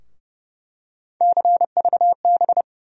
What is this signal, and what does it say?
Morse code, heard as a keyed tone, about 30 wpm, 700 Hz. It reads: CVB
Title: USA Call Sign Postfixes - 3 Letters 30wpm